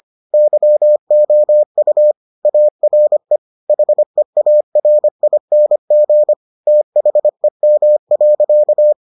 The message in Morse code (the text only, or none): YOU ARE HEARING THEM.